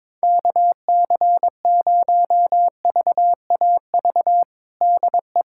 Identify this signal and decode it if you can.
Morse code, heard as a keyed tone, about 21 wpm, 700 Hz. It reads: KC0VAV DE
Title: USA Callsign DE Callsign 22wpm